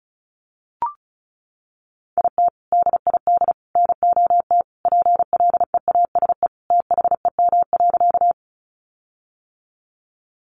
Morse code, heard as a keyed tone, about 34 wpm, 700 Hz. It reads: IT DID NOT PLEASE THEM.
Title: Sentences from Top 1500 Words 35wpm